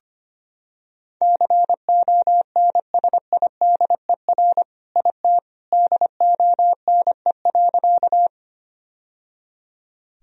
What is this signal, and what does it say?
Morse code, heard as a keyed tone, about 25 wpm, 700 Hz. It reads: CONSIDER IT DONE.
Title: Sentences from Top 1000 Words - Review Part 4 - 25wpm